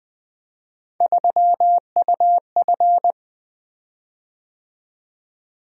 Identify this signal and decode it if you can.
Morse code, heard as a keyed tone, about 20 wpm, 700 Hz. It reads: 3UF
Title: Three Letter-Number Combinations - 20wpm